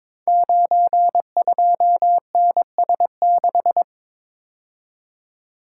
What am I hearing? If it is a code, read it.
Morse code, heard as a keyed tone, about 22 wpm, 700 Hz. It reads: 92NS6
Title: Code Groups - 5 Letter:Numbers - 22wpm